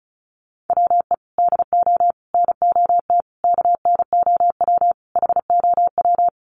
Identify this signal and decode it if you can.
Morse code, heard as a keyed tone, about 34 wpm, 700 Hz. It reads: WE DO NOT KNOW HOW
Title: Sets of 5 Words Encore 35wpm